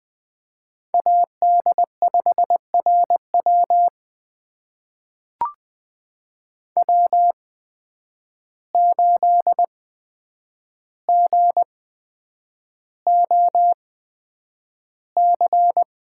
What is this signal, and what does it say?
Morse code, heard as a keyed tone, about 20 wpm, spaced as for 9 wpm, 700 Hz. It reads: AD5RW W8GOC
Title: USA Call Signs - 5 Characters - 8x Character Spacing - 20wpm